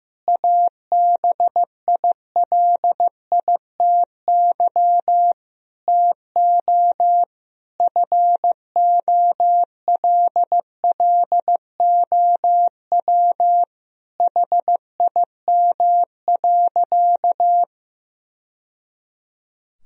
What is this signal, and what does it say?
Morse code, heard as a keyed tone, about 15 wpm, 700 Hz. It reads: ABILITY TO FOLLOW HIM.
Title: Sentences from Top 1000 Words - Review Part 3 - 15wpm